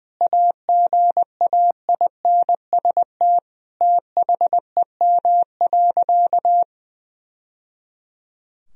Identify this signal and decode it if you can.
Morse code, heard as a keyed tone, about 20 wpm, 700 Hz. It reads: AGAINST THEM.
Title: Sentences from Top 1000 Words - Review Part 2 - 20wpm